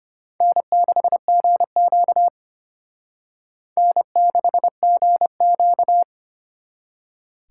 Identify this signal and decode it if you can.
Morse code, heard as a keyed tone, about 29 wpm, 700 Hz. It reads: N6GQ N6GQ
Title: Speed Racing - USA 1x2 Call Signs 25wpm